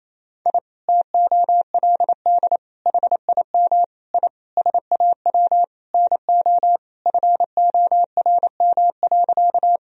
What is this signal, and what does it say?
Morse code, heard as a keyed tone, about 28 wpm, 700 Hz. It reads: I TOLD HIM I SAW NO FORM.
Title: Sentences from Top 1000 Words - Review Part 1 - 28wpm